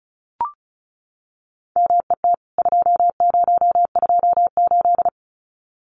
Morse code, heard as a keyed tone, about 34 wpm, 700 Hz. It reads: MET 2028
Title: CWT Exchange 35wpm